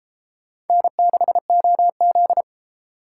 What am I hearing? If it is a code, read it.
Morse code, heard as a keyed tone, about 32 wpm, 700 Hz. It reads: N6OZ